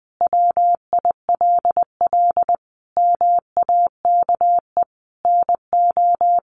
Morse code, heard as a keyed tone, about 20 wpm, 700 Hz. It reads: WILL MAKE NO
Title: Sentences from Top 800 Words 20wpm